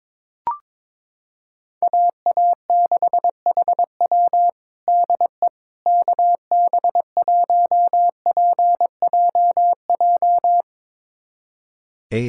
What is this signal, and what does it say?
Morse code, heard as a keyed tone, about 22 wpm, 700 Hz. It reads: AA6HW DE KB1PJJ
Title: USA Callsign DE Callsign 22wpm